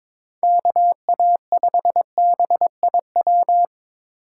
Morse code, heard as a keyed tone, about 22 wpm, 700 Hz. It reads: KA5BIW
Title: USA Call Signs - 6 Characters - 3x Character Spacing - 22wpm